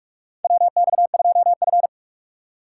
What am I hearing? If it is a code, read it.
Morse code, heard as a keyed tone, about 46 wpm, 700 Hz. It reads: WX2F
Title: USA Call Signs - 4 Characters - 4x Character Spacing - 45wpm